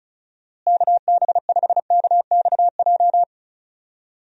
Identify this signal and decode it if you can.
Morse code, heard as a keyed tone, about 34 wpm, 700 Hz. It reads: KB5KXJ
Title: USA Call Signs - 6 Characters - Repeated 3x - 35wpm